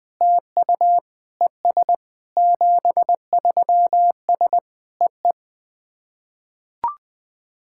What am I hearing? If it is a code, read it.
Morse code, heard as a keyed tone, about 20 wpm, 700 Hz. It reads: TU ES 73S EE
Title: SOTA QSO Practice 20wpm